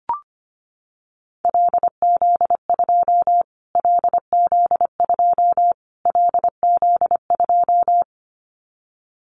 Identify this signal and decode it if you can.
Morse code, heard as a keyed tone, about 25 wpm, 700 Hz. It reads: LZ2 LZ2 LZ2
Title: Callsign Prefixes for Top 25 DXCC - Repeated 3x - 25wpm